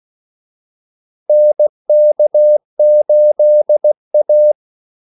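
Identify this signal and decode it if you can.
Morse code, heard as a keyed tone, about 16 wpm, 600 Hz. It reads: NK8A
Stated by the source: Morse call sign practice at 16 wpm.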